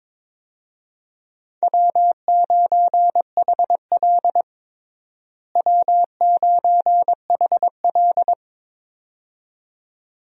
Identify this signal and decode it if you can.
Morse code, heard as a keyed tone, about 21 wpm, 700 Hz. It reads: W9HL W9HL
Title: USA Call Signs - 4 Characters - Repeated 2x - 22wpm